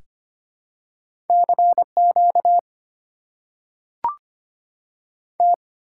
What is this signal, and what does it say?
Morse code, heard as a keyed tone, about 25 wpm, 700 Hz. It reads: C Q T
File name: Top 90 QSO Elements - 6x Character Spacing - 25wpm